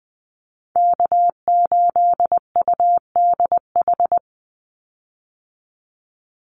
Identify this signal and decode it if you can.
Morse code, heard as a keyed tone, about 20 wpm, 700 Hz. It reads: K8UDH
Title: Call Signs Repeated 3x - 20wpm